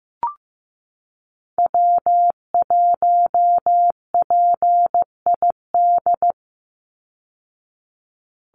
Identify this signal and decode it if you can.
Morse code, heard as a keyed tone, about 15 wpm, 700 Hz. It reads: W1PID
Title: Call Signs Repeated 3x - 15wpm